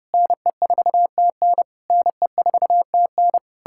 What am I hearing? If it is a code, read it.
Morse code, heard as a keyed tone, about 33 wpm, 700 Hz. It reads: NE4TN NE4TN